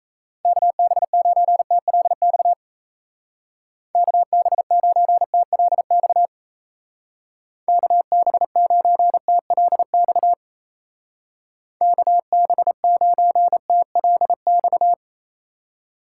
Morse code, heard as a keyed tone, about 34 wpm, 700 Hz. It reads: KB9TLX KB9TLX KB9TLX KB9TLX
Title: Speed Racing - USA 2x3 Call Signs 28wpm